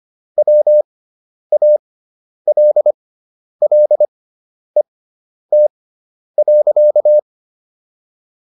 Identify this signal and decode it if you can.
Morse code, heard as a keyed tone, about 25 wpm, spaced as for 10 wpm, 600 Hz. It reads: WALLET.